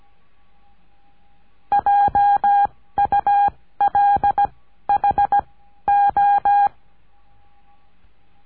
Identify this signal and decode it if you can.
Morse code, heard as a keyed tone, about 17 wpm, 810 Hz. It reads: J U L H O
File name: cursos / CURSO DE TELEGRAFIA / Telegrafia - Trilha 14